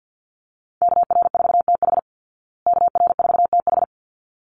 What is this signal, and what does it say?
Morse code, heard as a keyed tone, about 50 wpm, 700 Hz. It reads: KR4TH KR4TH
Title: Call Signs Repeated 3x - 50wpm